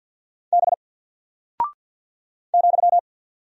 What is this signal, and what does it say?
Morse code, heard as a keyed tone, about 49 wpm, 700 Hz. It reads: B ,
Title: Single Character (Letter:Number:Punctuation:Prosign) 50wpm